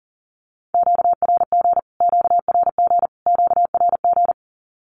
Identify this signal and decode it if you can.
Morse code, heard as a keyed tone, about 40 wpm, 700 Hz. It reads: QRG QRG QRG
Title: Q-Codes Repeated 3x 40wpm